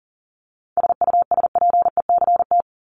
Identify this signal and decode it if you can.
Morse code, heard as a keyed tone, about 40 wpm, 700 Hz. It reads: SUSPECT
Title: Top 1751-2000 Words 40wpm